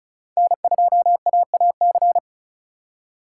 Morse code, heard as a keyed tone, about 36 wpm, 700 Hz. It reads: N2AAC
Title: USA Call Signs - 5 Characters - 2x Character Spacing - 35wpm